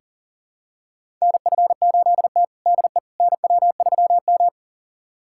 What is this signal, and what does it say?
Morse code, heard as a keyed tone, about 40 wpm, 700 Hz. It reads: NF8T DE NW3M
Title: USA Callsign DE Callsign - 4 Characters - 40wpm